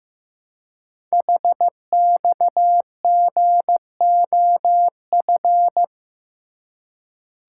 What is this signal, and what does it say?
Morse code, heard as a keyed tone, about 15 wpm, 700 Hz. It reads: HXGOF
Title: Code Groups - 5 Letters - 15wpm